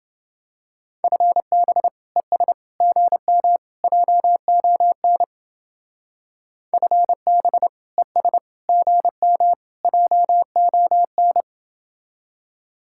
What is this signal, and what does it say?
Morse code, heard as a keyed tone, about 29 wpm, 700 Hz. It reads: FB ES GM JON FB ES GM JON